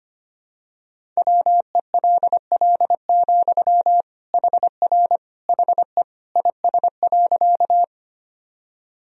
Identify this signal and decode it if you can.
Morse code, heard as a keyed tone, about 25 wpm, 700 Hz. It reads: WELL, HR HE IS.
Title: Sentences from Top 200 Words and Abbreviations 25wpm